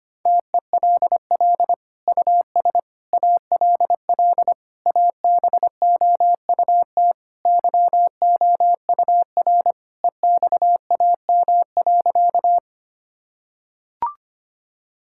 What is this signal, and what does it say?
Morse code, heard as a keyed tone, about 25 wpm, 700 Hz. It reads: TELL US ALL ABOUT YOUR EXAM.
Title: Sentences from Top 1000 Words - Review Part 5 - 25wpm